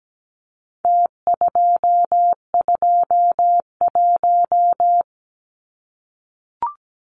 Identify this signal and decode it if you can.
Morse code, heard as a keyed tone, about 17 wpm, 700 Hz. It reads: T221
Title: Numbers - Contest Style - 4-Digit 17wpm